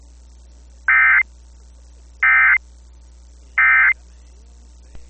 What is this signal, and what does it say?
Emergency Alert System (SAME) tones, heard as harsh buzzing bursts. It is heard from 0.9 to 3.9 s.